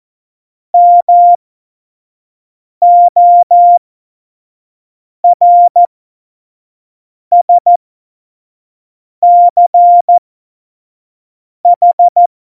Morse code, heard as a keyed tone, about 14 wpm, spaced as for 5 wpm, 700 Hz. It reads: MORSCH
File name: Morsekurs / Witze / Witz 18. September 2018, 25 F70